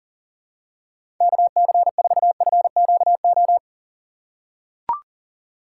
Morse code, heard as a keyed tone, about 40 wpm, 700 Hz. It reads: KC4FQO
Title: USA Call Signs - 6 Characters - 2x Character Spacing - 40wpm